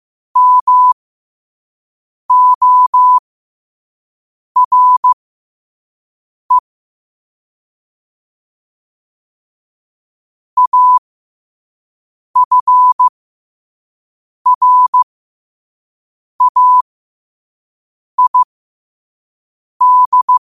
Morse code, heard as a keyed tone, about 15 wpm, spaced as for 5 wpm, 1000 Hz. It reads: MORE AFRAID
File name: 5wpm